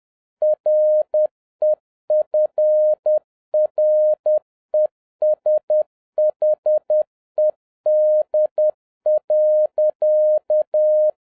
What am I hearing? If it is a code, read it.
Morse code, heard as a keyed tone, about 10 wpm, 610 Hz. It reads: REFRESHED.